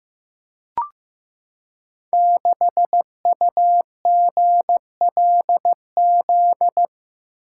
Morse code, heard as a keyed tone, about 15 wpm, 700 Hz. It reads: 6UGLZ
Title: Code Groups - 5 Letter:Numbers - 15wpm